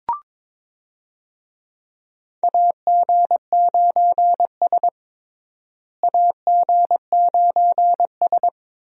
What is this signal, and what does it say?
Morse code, heard as a keyed tone, about 22 wpm, 700 Hz. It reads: AG9S AG9S
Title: USA Call Signs - 4 Characters - Repeated 2x - 22wpm